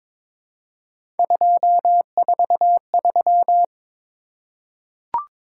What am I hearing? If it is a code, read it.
Morse code, heard as a keyed tone, about 21 wpm, 700 Hz. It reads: 243
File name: practice / 22wpm / Three Letter-Number Combinations - 22wpm